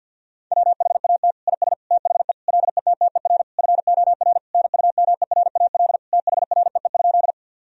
Morse code, heard as a keyed tone, about 50 wpm, 700 Hz. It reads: WHAT IS THE LETTER FOR NUMERAL THREE?
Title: Sentences from Top 1000 Words - Review Part 2 - 50wpm